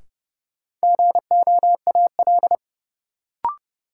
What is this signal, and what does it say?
Morse code, heard as a keyed tone, about 32 wpm, 700 Hz. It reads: GOAL
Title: Sentences from Top 1000 Words - Review Part 5 - New Word Drill - 30wpm